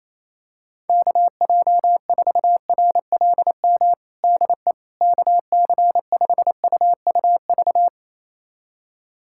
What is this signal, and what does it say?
Morse code, heard as a keyed tone, about 28 wpm, 700 Hz. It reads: KJ4RLM DE KC5UUV
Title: USA Callsign DE Callsign - 6 Characters - 28wpm